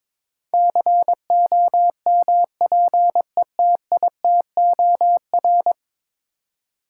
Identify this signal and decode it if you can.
Morse code, heard as a keyed tone, about 22 wpm, 700 Hz. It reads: COMPETITOR
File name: practice / 22wpm / 10 Letter Words - First 3 Letters Given - 22wpm